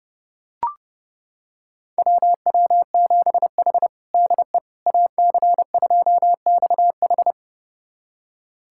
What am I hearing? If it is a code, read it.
Morse code, heard as a keyed tone, about 30 wpm, 700 Hz. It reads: WW7H DE AC2XH
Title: USA Callsign DE Callsign 30wpm